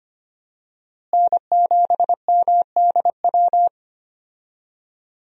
Morse code, heard as a keyed tone, about 25 wpm, 700 Hz. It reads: N7MDW
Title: USA Call Signs - 5 Characters - Repeated 2x - 25wpm